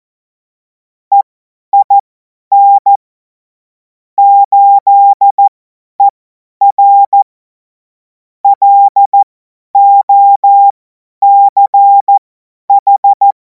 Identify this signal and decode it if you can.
Morse code, heard as a keyed tone, about 14 wpm, spaced as for 10 wpm, 800 Hz. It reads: EIN 8ER LOCH
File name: Morsekurs / Witze / Witz 31. März 2026 50 F70